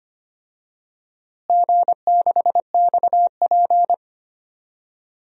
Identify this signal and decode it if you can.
Morse code, heard as a keyed tone, about 25 wpm, 700 Hz. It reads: G6XP
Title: DXCC-100 Call Signs - 4 Characters - Repeated 3x - 25wpm